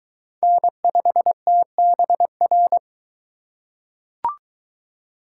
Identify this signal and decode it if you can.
Morse code, heard as a keyed tone, about 21 wpm, 700 Hz. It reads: N5TBR